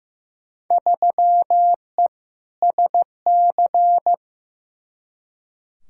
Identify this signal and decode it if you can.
Morse code, heard as a keyed tone, about 15 wpm, 700 Hz. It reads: 3E SC
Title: Field Day Exchanges Encore - 15wpm